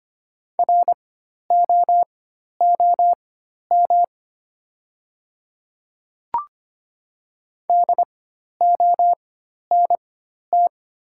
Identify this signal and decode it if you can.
Morse code, heard as a keyed tone, about 25 wpm, spaced as for 12 wpm, 700 Hz. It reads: ROOM DONT
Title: 4x Character Spacing - Top 300 Words - 25wpm